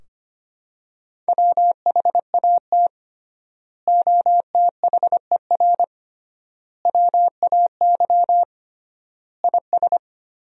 Morse code, heard as a keyed tone, about 25 wpm, 700 Hz. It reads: WHAT OTHER WAY IS
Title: 3x Word-Spacing - Sentences from Top 100 Words - 25wpm